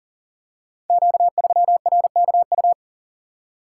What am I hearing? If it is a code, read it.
Morse code, heard as a keyed tone, about 39 wpm, 690 Hz. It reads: Q3RKU
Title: Code Groups - 5 Letter:Numbers - 40wpm